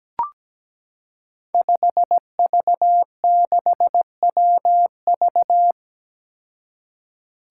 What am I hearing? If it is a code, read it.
Morse code, heard as a keyed tone, about 17 wpm, 700 Hz. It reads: 5V6WV
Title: Five Challenging Letter-Number Combinations 17wpm